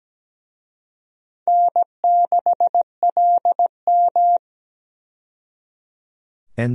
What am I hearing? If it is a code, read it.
Morse code, heard as a keyed tone, about 17 wpm, 700 Hz. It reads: N6LM
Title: USA Call Signs - 4 Characters - 17wpm